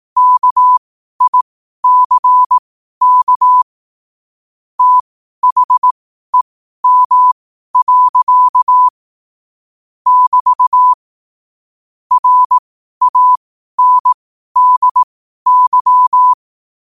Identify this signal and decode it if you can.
Morse code, heard as a keyed tone, about 18 wpm, spaced as for 12 wpm, 1000 Hz. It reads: KICK THEM. <BT> RANDY
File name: 12wpm